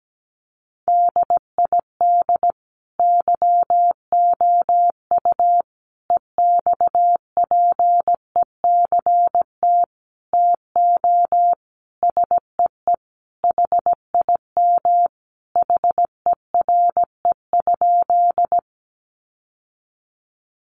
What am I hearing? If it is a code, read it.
Morse code, heard as a keyed tone, about 17 wpm, 700 Hz. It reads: DID YOU EXPECT TO SEE HIM HERE?
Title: Sentences from Top 1100 Words 17wpm